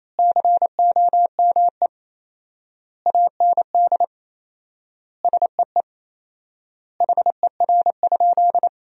Morse code, heard as a keyed tone, about 28 wpm, 700 Hz. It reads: COME AND SEE HER?
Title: 4x Word-Spacing - Sentences from Top 100 Words - 28wpm